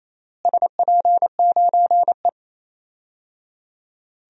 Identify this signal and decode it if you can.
Morse code, heard as a keyed tone, about 28 wpm, 700 Hz. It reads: SP9E
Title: DXCC-100 Call Signs - 4 Characters - Repeated 3x - 28wpm